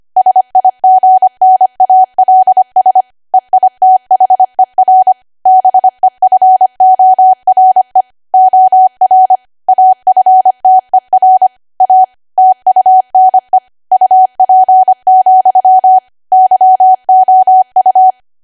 Morse code, heard as a keyed tone, about 25 wpm, 750 Hz. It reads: SIGNALS EITHER BEFORE OR AFTER A TUNE UP, YOU
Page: 25 WPM Code Archive